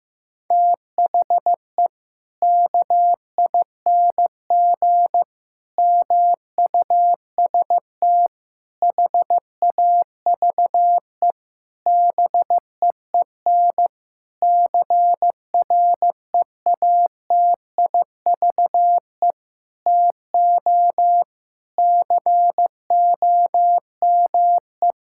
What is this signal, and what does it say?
Morse code, heard as a keyed tone, about 16 wpm, 700 Hz. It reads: THE KING MUST HAVE BEEN CREATIVE TO COME
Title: Sentences from Top 1000 Words - Review Part 5 - 15wpm